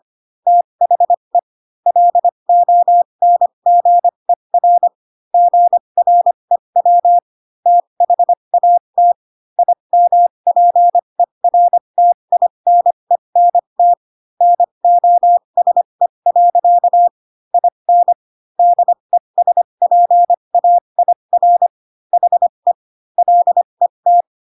Morse code, heard as a keyed tone, about 25 wpm, spaced as for 22 wpm, 700 Hz. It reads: THE LONGER GREW THAT IMPERTINENT NOSE. IN DESPAIR HE LET
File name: Pinocchio / Pinocchio ch3 18 25 ik1hge